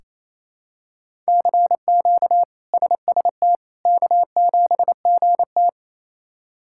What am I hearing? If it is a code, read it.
Morse code, heard as a keyed tone, about 28 wpm, 700 Hz. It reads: CQ SST K7GT
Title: K1USN SST QSO Practice - 28wpm